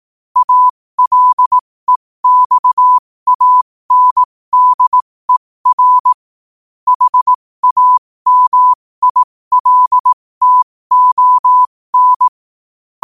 Morse code, heard as a keyed tone, about 18 wpm, spaced as for 15 wpm, 1000 Hz. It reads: ALEXANDER HAMILTON
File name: qotdfiles / 15wpm